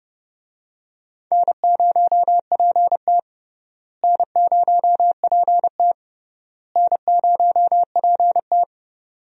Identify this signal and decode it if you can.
Morse code, heard as a keyed tone, about 32 wpm, 700 Hz. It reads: N0PT N0PT N0PT
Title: USA Call Signs - 4 Characters - Repeated 3x - 30wpm